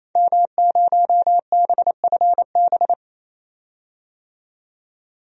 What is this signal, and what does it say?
Morse code, heard as a keyed tone, about 28 wpm, 700 Hz. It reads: M0BFB
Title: DXCC-100 Call Signs - 5 Characters - 28wpm